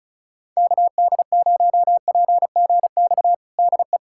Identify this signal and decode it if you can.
Morse code, heard as a keyed tone, about 36 wpm, 700 Hz. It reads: KD0PGX DE
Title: USA Callsign DE Callsign 35wpm